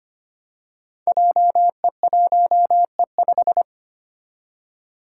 Morse code, heard as a keyed tone, about 25 wpm, 700 Hz. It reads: JE1E5
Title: Five Challenging Letter-Number Combinations 25wpm